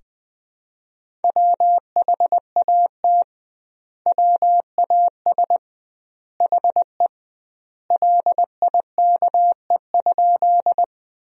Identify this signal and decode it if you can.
Morse code, heard as a keyed tone, about 21 wpm, 700 Hz. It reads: WHAT WAS HE LIKE?
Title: 2x Word-Spacing - Sentences from Top 100 Words - 20wpm